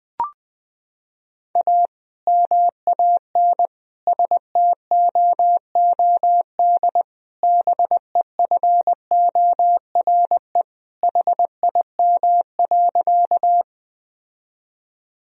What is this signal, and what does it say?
Morse code, heard as a keyed tone, about 21 wpm, 700 Hz. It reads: A MAN STOOD BEFORE HIM.
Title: Sentences from Top 1000 Words - Review Part 2 - 20wpm